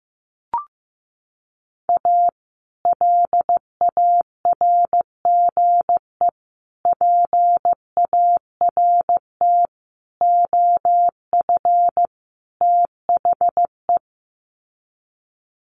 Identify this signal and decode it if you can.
Morse code, heard as a keyed tone, about 15 wpm, 700 Hz. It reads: A LARGE PART OF THE
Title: Sets of 5 Words Encore2 15wpm